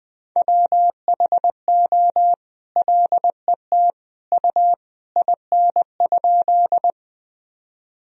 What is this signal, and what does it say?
Morse code, heard as a keyed tone, about 20 wpm, 700 Hz. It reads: WHO LET U IN?
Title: Sentences from Top 300 Words and Abbreviations 20wpm